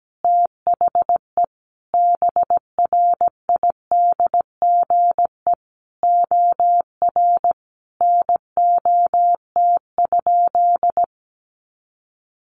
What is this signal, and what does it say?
Morse code, heard as a keyed tone, about 17 wpm, 700 Hz. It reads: THE BRIDGE OR NOT?
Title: Sentences from Top 1300 Words 17wpm